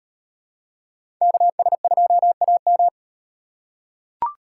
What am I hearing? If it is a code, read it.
Morse code, heard as a keyed tone, about 36 wpm, 700 Hz. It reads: KS2AM